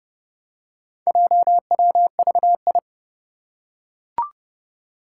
Morse code, heard as a keyed tone, about 32 wpm, 700 Hz. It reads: JWVI
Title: Four Challenging Letter-Number Combinations 30wpm